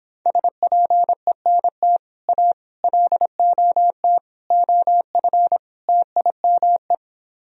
Morse code, heard as a keyed tone, about 26 wpm, 700 Hz. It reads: SPENT A LOT OF TIME